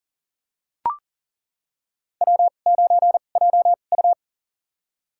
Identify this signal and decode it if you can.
Morse code, heard as a keyed tone, about 40 wpm, spaced as for 29 wpm, 700 Hz. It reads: W9JU
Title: USA Call Signs - 2x Character Spacing - 40wpm